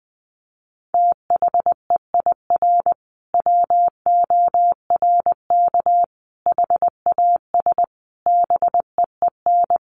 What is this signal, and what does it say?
Morse code, heard as a keyed tone, about 20 wpm, 700 Hz. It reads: THEIR WORK HAS BEEN
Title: Sentences from Top 1500 Words 20wpm